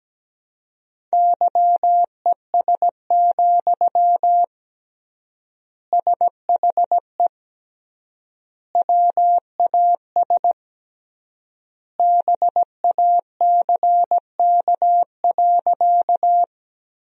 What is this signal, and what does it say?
Morse code, heard as a keyed tone, about 17 wpm, 700 Hz. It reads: YES, SHE WAS BACK.
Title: 3x Word-Spacing - Sentences from Top 500 Words - 17wpm